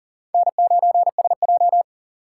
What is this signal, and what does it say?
Morse code, heard as a keyed tone, about 40 wpm, 700 Hz. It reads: N9SJ
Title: USA Call Signs - 4 Characters - 8x Character Spacing - 40wpm